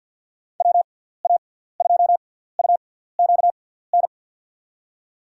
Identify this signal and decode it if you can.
Morse code, heard as a keyed tone, about 51 wpm, 700 Hz. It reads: W A 2 U Q N
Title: USA Call Signs - 6 Characters - 6x Character Spacing - 50wpm